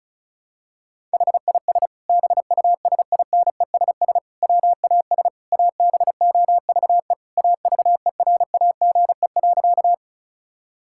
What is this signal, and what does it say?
Morse code, heard as a keyed tone, about 36 wpm, 700 Hz. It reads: HIS BUSINESS WAS ABOVE AVERAGE.
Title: Sentences from Top 1000 Words - Review Part 6 - 35wpm